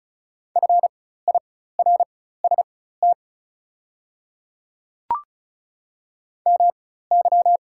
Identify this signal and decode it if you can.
Morse code, heard as a keyed tone, about 34 wpm, spaced as for 16 wpm, 690 Hz. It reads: FIRST MY
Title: 4x Character Spacing - Top 300 Words - 35wpm